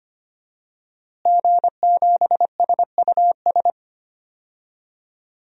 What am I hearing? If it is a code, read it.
Morse code, heard as a keyed tone, about 25 wpm, 700 Hz. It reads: G7SUS
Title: DXCC-100 Call Signs - 5 Characters - Repeated 3x - 25wpm